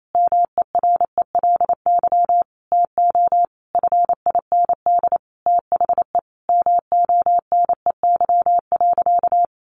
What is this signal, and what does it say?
Morse code, heard as a keyed tone, about 28 wpm, 700 Hz. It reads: MERELY TO FIND THE MONEY.
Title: Sentences from Top 1400 Words 28wpm